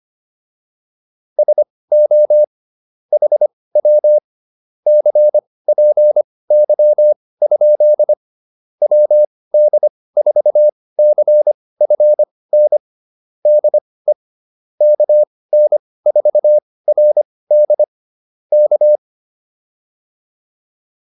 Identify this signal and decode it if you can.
Morse code, heard as a keyed tone, about 25 wpm, spaced as for 18 wpm, 600 Hz. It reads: SO HW CPY? WD4CFN DE KN4RD K
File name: QSO207 18